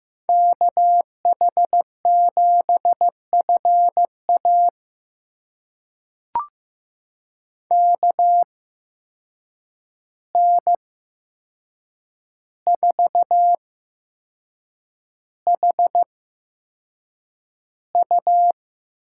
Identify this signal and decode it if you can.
Morse code, heard as a keyed tone, about 15 wpm, spaced as for 6 wpm, 700 Hz. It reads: KH7FA KN4HU
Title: USA Call Signs - 8x Character Spacing - 15wpm